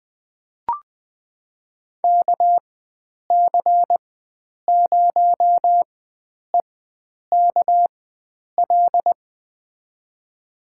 Morse code, heard as a keyed tone, about 20 wpm, spaced as for 9 wpm, 700 Hz. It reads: KC0EKL
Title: USA Call Signs - 4x Character Spacing - 20wpm